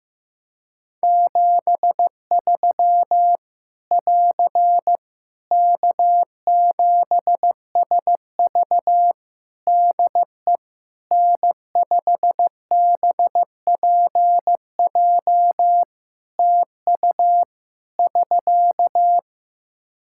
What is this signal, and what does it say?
Morse code, heard as a keyed tone, about 15 wpm, 700 Hz. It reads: 73 <AR> K7SV DE N5BPJ TU <SK>
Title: Practice QSO - Exchange 4 - Ending - 15wpm